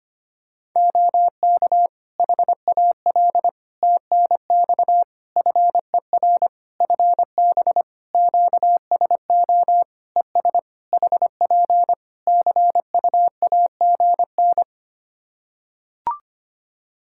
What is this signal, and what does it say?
Morse code, heard as a keyed tone, about 25 wpm, 700 Hz. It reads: OK HAL TNX FER FB QSO ES HP CUAGN
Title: Practice QSO - Exchange 4 - Ending - 25wpm